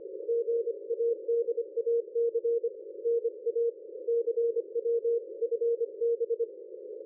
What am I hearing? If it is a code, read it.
Morse code, heard as a keyed tone, about 25 wpm, 470 Hz. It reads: GADAC NA CW FB
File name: dlazaawansowanych / nagraniazpasma / ExpertSDR2 line out 23.01.2020 14.04.01